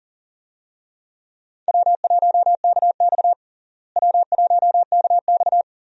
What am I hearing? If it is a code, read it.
Morse code, heard as a keyed tone, about 40 wpm, 700 Hz. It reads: W1KX W1KX
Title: USA Call Signs - 4 Characters - Repeated 2x - 40wpm